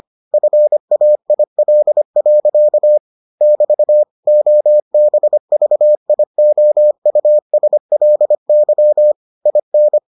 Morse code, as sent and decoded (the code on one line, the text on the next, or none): ..-. .- .. .-.. .-.-.- / -...- / --- -... ...- .. --- ..- ... .-.. -.-- / .. -.
FAIL. <BT> OBVIOUSLY IN